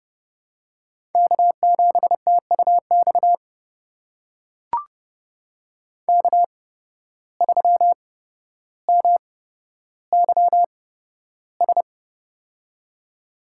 Morse code, heard as a keyed tone, about 30 wpm, spaced as for 13 wpm, 700 Hz. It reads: K7TUX K3MYS
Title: USA Call Signs - 5 Characters - 8x Character Spacing - 30wpm